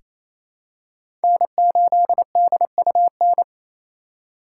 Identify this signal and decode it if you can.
Morse code, heard as a keyed tone, about 28 wpm, 700 Hz. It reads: N8DUN
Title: USA Call Signs - 4x Character Spacing - 28wpm